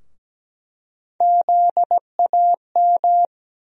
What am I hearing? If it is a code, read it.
Morse code, heard as a keyed tone, about 17 wpm, 700 Hz. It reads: ZAM
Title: USA Call Sign Postfixes - 3 Letters 17wpm